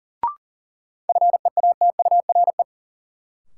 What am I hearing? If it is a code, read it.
Morse code, heard as a keyed tone, about 37 wpm, 700 Hz. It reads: FEATURE
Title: Sentences from Top 800 Words - New Word Drill - 40wpm